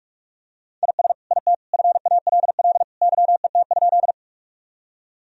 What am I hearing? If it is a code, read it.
Morse code, heard as a keyed tone, about 41 wpm, 700 Hz. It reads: IS IT FALL YET?
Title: Sentences from Top 1000 Words - Review Part 2 - 45wpm